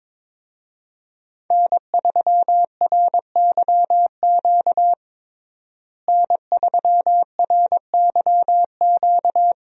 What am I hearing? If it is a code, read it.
Morse code, heard as a keyed tone, about 22 wpm, 700 Hz. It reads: N3RYQ N3RYQ
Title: USA Call Signs - 5 Characters - Repeated 2x - 22wpm